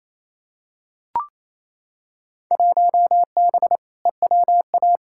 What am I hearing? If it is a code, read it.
Morse code, heard as a keyed tone, about 28 wpm, 700 Hz. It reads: 1B EWA
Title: Field Day Exchanges Encore - 28wpm